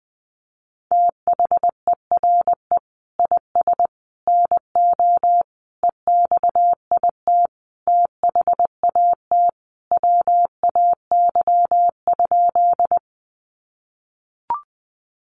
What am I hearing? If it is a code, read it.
Morse code, heard as a keyed tone, about 20 wpm, 700 Hz. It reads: THERE IS NO EXIT THAT WAY?
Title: Sentences from Top 1100 Words 20wpm